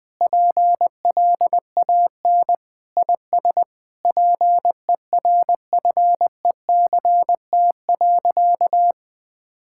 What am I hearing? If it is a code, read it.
Morse code, heard as a keyed tone, about 20 wpm, 700 Hz. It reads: PLAN IS PERFECT.
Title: Sentences with 7-Word Phrases - 20wpm